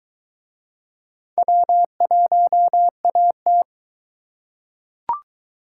Morse code, heard as a keyed tone, about 22 wpm, 700 Hz. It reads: W1AT